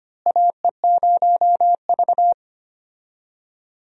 Morse code, heard as a keyed tone, about 25 wpm, 700 Hz. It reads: AE0V
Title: USA Call Signs - 4 Characters - Repeated 3x - 25wpm